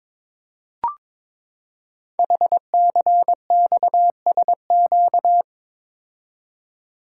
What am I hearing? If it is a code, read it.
Morse code, heard as a keyed tone, about 22 wpm, 700 Hz. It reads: HCXSQ
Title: Code Groups - 5 Letters - 22wpm